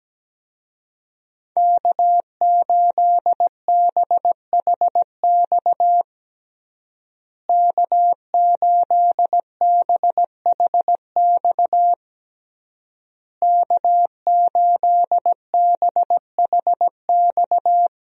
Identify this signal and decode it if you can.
Morse code, heard as a keyed tone, about 17 wpm, 700 Hz. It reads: K8BHX K8BHX K8BHX
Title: USA Call Signs - 5 Characters - Repeated 3x - 17wpm